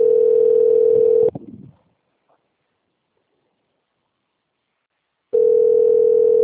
Ringback tone (North America), 440 + 480 Hz.
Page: Sao Carlos